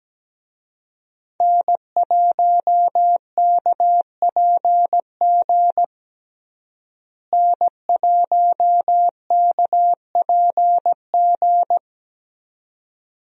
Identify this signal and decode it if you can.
Morse code, heard as a keyed tone, about 17 wpm, 700 Hz. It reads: N1KPG N1KPG
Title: USA Call Signs - 5 Characters - Repeated 2x - 17wpm